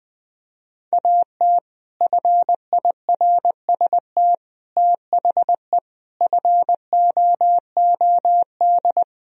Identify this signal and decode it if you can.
Morse code, heard as a keyed tone, about 20 wpm, 700 Hz. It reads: AT FIRST THE FOOD
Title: Sentences from Top 1000 Words - Review Part 1 - 20wpm